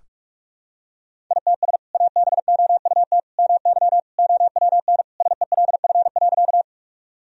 Morse code, heard as a keyed tone, about 41 wpm, 700 Hz. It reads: ITS ABOUT MY OWN SELF.
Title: Sentences from Top 1000 Words - Review Part 1 - 45wpm